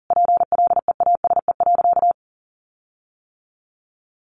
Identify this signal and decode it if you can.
Morse code, heard as a keyed tone, about 40 wpm, 700 Hz. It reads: PLEASE.
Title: Sentences from Top 1500 Words 40wpm